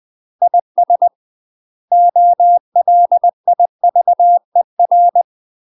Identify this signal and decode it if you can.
Morse code, heard as a keyed tone, about 20 wpm, 700 Hz. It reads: IS OLIVER
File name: UK MASTER QSOs TRAINING / 20WPM / UK MASTER QSOs 0011